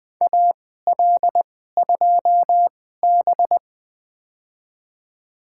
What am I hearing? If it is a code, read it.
Morse code, heard as a keyed tone, about 20 wpm, spaced as for 15 wpm, 700 Hz. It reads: AL2B
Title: USA Call Signs - 4 Characters - 2x Character Spacing - 20wpm